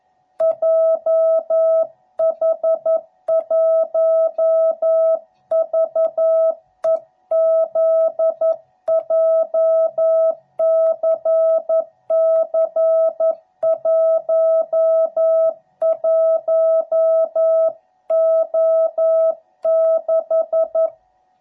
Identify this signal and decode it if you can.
Morse code, heard as a keyed tone, about 11 wpm, 660 Hz. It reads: JH1VEZ JCC11O6
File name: jh8cbh / 2021cw / cw12b